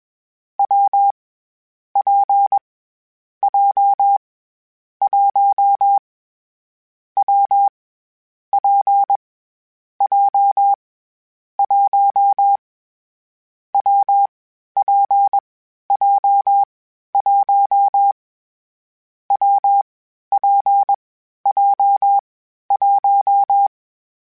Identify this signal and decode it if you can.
Morse code, heard as a keyed tone, about 21 wpm, 800 Hz. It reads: W P J 1 W P J 1 W P J 1 W P J 1
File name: Difficult Combos / Difficult Combos-mp3s / WPJ WPJ1 @20